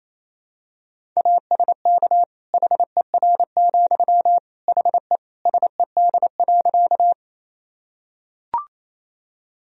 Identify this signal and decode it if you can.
Morse code, heard as a keyed tone, about 27 wpm, 700 Hz. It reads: ASK HER, HE SED.
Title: Sentences from Top 200 Words and Abbreviations 28wpm